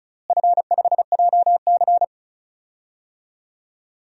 Morse code, heard as a keyed tone, about 36 wpm, 700 Hz. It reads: F5JC
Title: DXCC-100 Call Signs - 4 Characters - Repeated 3x - 35wpm